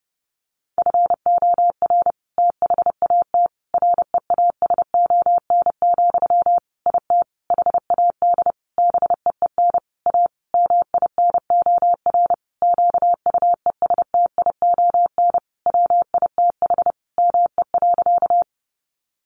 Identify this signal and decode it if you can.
Morse code, heard as a keyed tone, about 28 wpm, 700 Hz. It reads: FOR THAT REASON, IT HAD BEEN A MINOR QUESTION WITH ME.
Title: Sentences from Top 1500 Words 30wpm